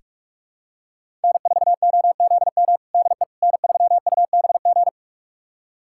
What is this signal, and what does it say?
Morse code, heard as a keyed tone, about 41 wpm, 700 Hz. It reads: N4OZM DE N3UBG
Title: USA Callsign DE Callsign - 5 Characters - 45wpm